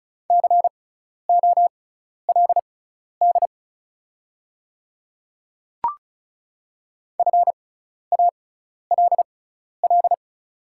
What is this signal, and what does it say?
Morse code, heard as a keyed tone, about 36 wpm, spaced as for 12 wpm, 690 Hz. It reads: COLD FALL
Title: 6x Character Spacing - Top 500 Words - 35wpm